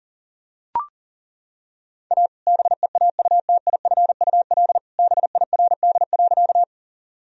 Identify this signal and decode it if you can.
Morse code, heard as a keyed tone, about 37 wpm, 700 Hz. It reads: A BEAUTIFUL BIRD.
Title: Sentences from Top 1000 Words - Review Part 2 - 40wpm